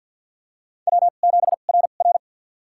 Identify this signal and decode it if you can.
Morse code, heard as a keyed tone, about 50 wpm, spaced as for 36 wpm, 700 Hz. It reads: W7RR
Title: USA Call Signs - 4 Characters - 2x Character Spacing - 50wpm